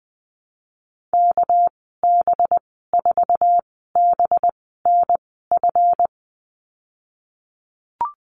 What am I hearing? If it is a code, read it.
Morse code, heard as a keyed tone, about 20 wpm, spaced as for 15 wpm, 700 Hz. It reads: KB4BNF
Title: 219 Lesson 35g - Number 0 - Call Signs 20wpm